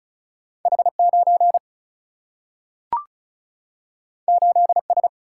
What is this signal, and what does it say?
Morse code, heard as a keyed tone, about 36 wpm, 700 Hz. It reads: H 9 8 S
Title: Single-Letter Single-Number 35wpm